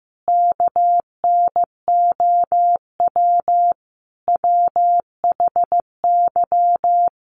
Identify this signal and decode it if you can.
Morse code, heard as a keyed tone, about 15 wpm, 700 Hz. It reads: KNOW WHY
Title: Sets of 4 Words 15wpm